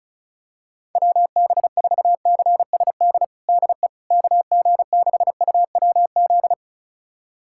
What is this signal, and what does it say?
Morse code, heard as a keyed tone, about 36 wpm, 700 Hz. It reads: WB4CSD DE KG6UWZ
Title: USA Callsign DE Callsign 35wpm